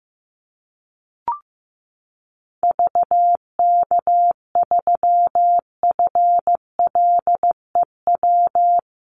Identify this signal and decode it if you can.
Morse code, heard as a keyed tone, about 14 wpm, 700 Hz. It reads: VK3FLEW
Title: USA Call Signs Repeated 3x 15wpm